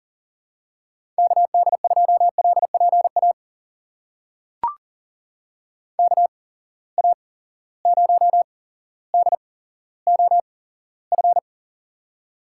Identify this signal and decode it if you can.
Morse code, heard as a keyed tone, about 38 wpm, spaced as for 17 wpm, 700 Hz. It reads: KD2LPA KA0DOF
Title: USA Call Signs - 6 Characters - 8x Character Spacing - 40wpm